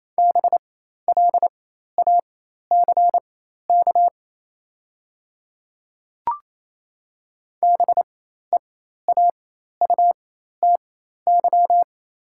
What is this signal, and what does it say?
Morse code, heard as a keyed tone, about 28 wpm, spaced as for 13 wpm, 700 Hz. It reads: BLACK BEAUTY
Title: 4x Character Spacing - Top 500 Words - 28wpm